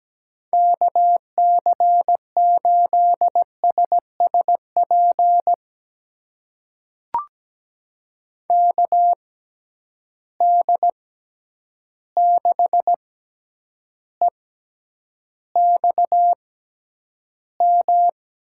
Morse code, heard as a keyed tone, about 17 wpm, spaced as for 9 wpm, 700 Hz. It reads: KC8SSP KD6EXM
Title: USA Call Signs - 6 Characters - 6x Character Spacing - 17wpm